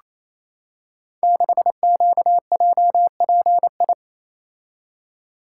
Morse code, heard as a keyed tone, about 27 wpm, 700 Hz. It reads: 6QJPI
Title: Code Groups - 5 Letter:Numbers - 28wpm